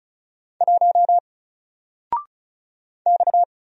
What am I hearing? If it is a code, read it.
Morse code, heard as a keyed tone, about 36 wpm, 700 Hz. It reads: 1 X
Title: Single Letter-Number 35wpm